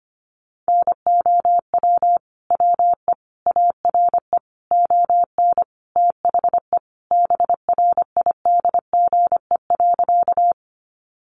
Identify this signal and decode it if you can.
Morse code, heard as a keyed tone, about 25 wpm, 700 Hz. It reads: NOW WE ARE ON THE BRIDGE.
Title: Sentences from Top 1200 Words 25wpm